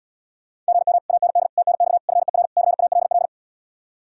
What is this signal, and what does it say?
Morse code, heard as a keyed tone, about 80 wpm, 700 Hz. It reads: GM AND TNX FER CALL